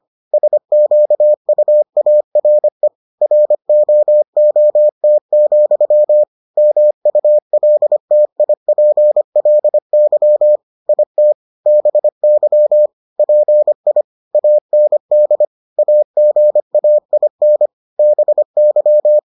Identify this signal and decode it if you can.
Morse code, heard as a keyed tone, about 25 wpm, 600 Hz. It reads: SQUARE ROOT, MULTIPLY IT BY PI AND AGAIN BY